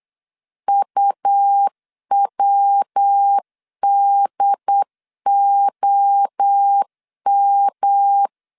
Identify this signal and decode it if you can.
Morse code, heard as a keyed tone, about 8 wpm, 800 Hz. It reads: UWDOM